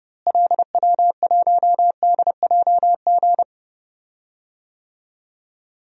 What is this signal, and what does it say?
Morse code, heard as a keyed tone, about 32 wpm, 700 Hz. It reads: LW1DJG
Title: DXCC-100 Call Signs - 6 Characters - Repeated 2x - 30wpm